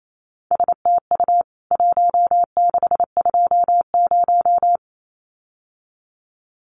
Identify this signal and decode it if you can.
Morse code, heard as a keyed tone, about 28 wpm, 700 Hz. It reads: STU 1620
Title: CWT Exchange 28wpm